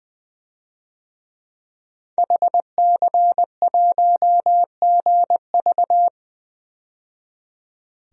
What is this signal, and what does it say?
Morse code, heard as a keyed tone, about 21 wpm, 700 Hz. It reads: HC1GV
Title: DXCC-100 Call Signs - 5 Characters - 20wpm